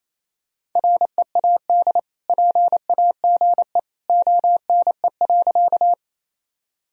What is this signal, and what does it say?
Morse code, heard as a keyed tone, about 28 wpm, 700 Hz. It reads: READ PAGE ONE.
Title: Sentences from Top 1000 Words - Review Part 1 - 28wpm